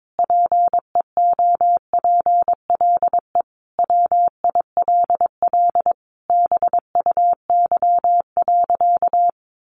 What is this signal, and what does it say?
Morse code, heard as a keyed tone, about 22 wpm, 700 Hz. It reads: PEOPLE WILL BUY.
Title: Sentences from Top 1300 Words 22wpm